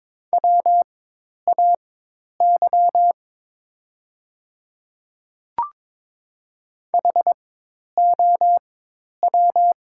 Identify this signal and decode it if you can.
Morse code, heard as a keyed tone, about 22 wpm, spaced as for 10 wpm, 700 Hz. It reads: WAY HOW
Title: 4x Character Spacing - Top 100 Words - 22wpm